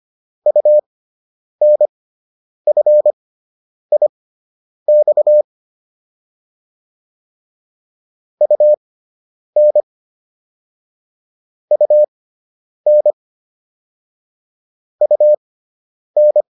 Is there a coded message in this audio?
Yes: Morse code from 0.5 to 16.4 s, heard as a keyed tone, about 25 wpm, spaced as for 9 wpm, 600 Hz. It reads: UNFIX UN UN UN